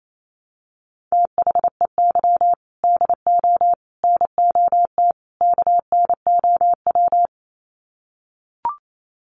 Morse code, heard as a keyed tone, about 28 wpm, 700 Hz. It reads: THEY DO NOT KNOW
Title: Sets of 4 Words 28wpm